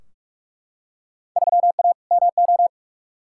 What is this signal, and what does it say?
Morse code, heard as a keyed tone, about 44 wpm, 700 Hz. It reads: 3A MO
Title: Field Day Exchanges - 45wpm